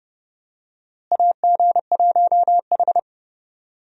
Morse code, heard as a keyed tone, about 30 wpm, 700 Hz. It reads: AG1H
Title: USA Call Signs - 4 Characters - Repeated 2x - 30wpm